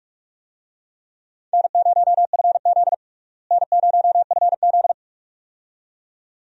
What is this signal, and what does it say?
Morse code, heard as a keyed tone, about 46 wpm, 700 Hz. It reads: N0FZ N0FZ
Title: USA Call Signs - 4 Characters - Repeated 2x - 45wpm